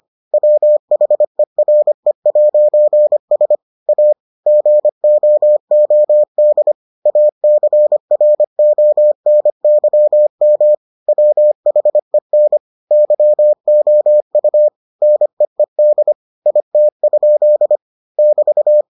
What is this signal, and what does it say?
Morse code, heard as a keyed tone, about 25 wpm, 600 Hz. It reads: WHERE'S A GOOD ACRONYM WHEN YOU NEED IT? <BT>